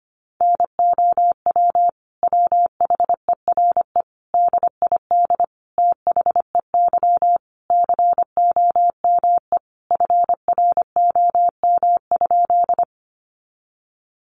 Morse code, heard as a keyed tone, about 25 wpm, 700 Hz. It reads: NOW WHERE DID THEY COME FROM?
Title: Sentences from Top 200 Words 25wpm